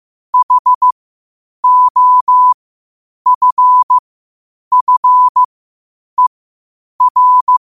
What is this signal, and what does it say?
Morse code, heard as a keyed tone, about 15 wpm, spaced as for 8 wpm, 1000 Hz. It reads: HOFFER
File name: qotdfiles / 8wpm